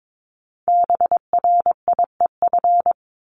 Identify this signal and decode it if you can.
Morse code, heard as a keyed tone, about 22 wpm, 700 Hz. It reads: BRIEF
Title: LY Suffix - Top 100 Words with Base Words 22wpm